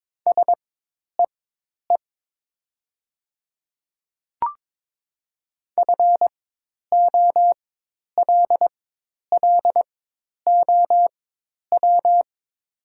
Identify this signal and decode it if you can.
Morse code, heard as a keyed tone, about 22 wpm, spaced as for 10 wpm, 700 Hz. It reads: SEE FOLLOW
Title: 4x Character Spacing - Top 200 Words - 22wpm